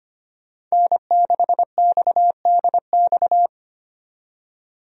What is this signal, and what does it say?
Morse code, heard as a keyed tone, about 25 wpm, 690 Hz. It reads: N6XDX
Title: USA Call Signs - 5 Characters - 25wpm